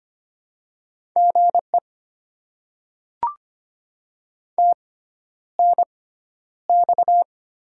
Morse code, heard as a keyed tone, about 25 wpm, spaced as for 11 wpm, 700 Hz. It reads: GE TNX
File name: Top 20 QSO Elements - 6x Character Spacing - 25wpm